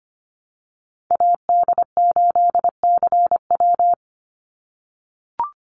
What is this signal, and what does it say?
Morse code, heard as a keyed tone, about 25 wpm, 700 Hz. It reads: AD8CW
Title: Call Signs - 25wpm